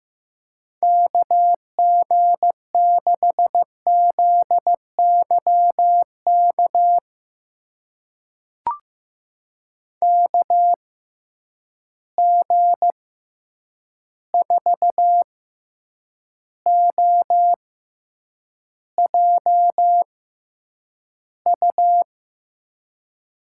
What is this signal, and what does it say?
Morse code, heard as a keyed tone, about 16 wpm, spaced as for 8 wpm, 700 Hz. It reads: KG6ZYK KG4OJU
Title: USA Call Signs - 6 Characters - 6x Character Spacing - 15wpm